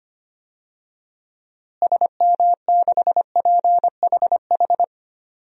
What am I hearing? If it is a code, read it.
Morse code, heard as a keyed tone, about 25 wpm, 700 Hz. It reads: SM6PHH
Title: DXCC-100 Call Signs - 6 Characters - Repeated 2x - 25wpm